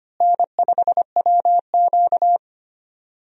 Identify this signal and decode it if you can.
Morse code, heard as a keyed tone, about 25 wpm, 700 Hz. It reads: N5WQ
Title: USA Call Signs - 4 Characters - Repeated 2x - 25wpm